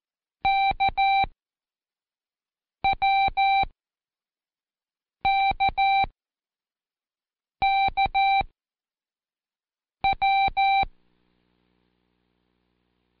Morse code, heard as a keyed tone, about 13 wpm, 780 Hz. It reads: K W K K W
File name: Code Quick / AudioDownld1 / Seg06-Day1-Intro K